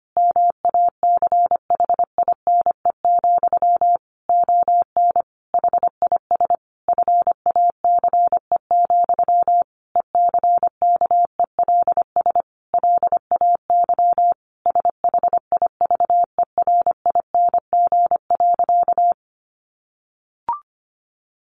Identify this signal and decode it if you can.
Morse code, heard as a keyed tone, about 25 wpm, 700 Hz. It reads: MACHINE, ON HIS FACE, ECKELS LAY SHIVERING.
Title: A Sound Of Thunder 25wpm